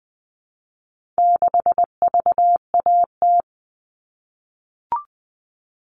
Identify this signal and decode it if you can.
Morse code, heard as a keyed tone, about 20 wpm, 700 Hz. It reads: 6VAT
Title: All Cut Numbers - 4-Digit 20wpm